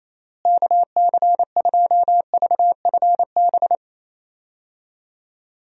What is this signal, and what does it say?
Morse code, heard as a keyed tone, about 28 wpm, 700 Hz. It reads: KC2VFB
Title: USA Call Signs - 6 Characters - Repeated 2x - 28wpm